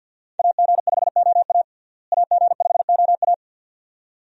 Morse code, heard as a keyed tone, about 52 wpm, 700 Hz. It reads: AG5OA AG5OA
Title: USA Call Signs - 5 Characters - Repeated 2x - 50wpm